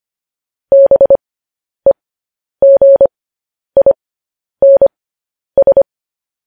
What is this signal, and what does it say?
Morse code, heard as a keyed tone, about 25 wpm, spaced as for 10 wpm, 550 Hz. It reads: BEGINS
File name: podcast / News headlines-1x-25x10